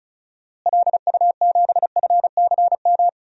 Morse code, heard as a keyed tone, about 34 wpm, 700 Hz. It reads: LU7FCM
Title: DXCC-100 Call Signs - 6 Characters - Repeated 2x - 35wpm